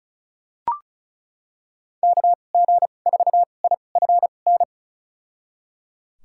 Morse code, heard as a keyed tone, about 36 wpm, spaced as for 25 wpm, 700 Hz. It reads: KG4IFN
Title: USA Call Signs - 6 Characters - 2x Character Spacing - 35wpm